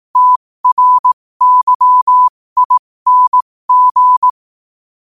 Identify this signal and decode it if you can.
Morse code, heard as a keyed tone, about 18 wpm, spaced as for 15 wpm, 1000 Hz. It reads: TRYING
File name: qotdfiles / 15wpm